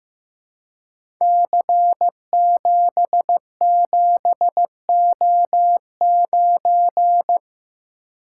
Morse code, heard as a keyed tone, about 15 wpm, 700 Hz. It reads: C77O9
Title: Code Groups - 5 Letter:Numbers - 15wpm